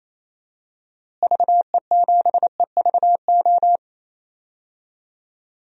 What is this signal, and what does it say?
Morse code, heard as a keyed tone, about 28 wpm, 700 Hz. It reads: VE7EVO
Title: DXCC-100 Call Signs - 6 Characters - Repeated 2x - 28wpm